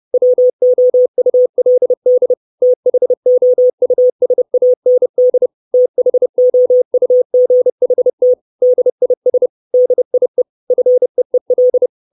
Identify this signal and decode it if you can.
Morse code, heard as a keyed tone, about 30 wpm, 500 Hz. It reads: WOULD THOUSAND THOUGHT DIS DIE FEEL
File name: 632 Words 30WPM